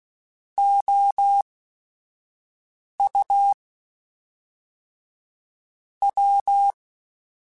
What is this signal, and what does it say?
Morse code, heard as a keyed tone, about 16 wpm, spaced as for 5 wpm, 800 Hz. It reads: OU W